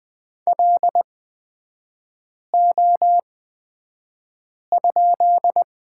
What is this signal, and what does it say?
Morse code, heard as a keyed tone, about 20 wpm, spaced as for 5 wpm, 700 Hz. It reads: LO?
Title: 224 Lesson 36e - BT - Three Characters 20wpm